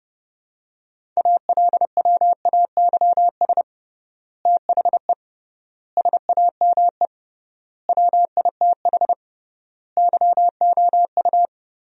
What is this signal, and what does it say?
Morse code, heard as a keyed tone, about 30 wpm, 700 Hz. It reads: ALWAYS THE SAME WITH YOU
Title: 3x Word-Spacing - Sentences from Top 300 Words - 30wpm